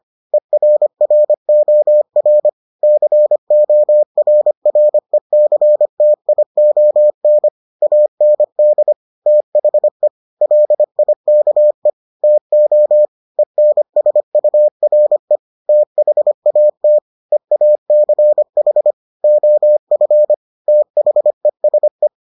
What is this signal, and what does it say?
Morse code, heard as a keyed tone, about 25 wpm, 600 Hz. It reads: ERROR CORRECTION AND THE LIKE TO ENSURE THAT EACH OF THESE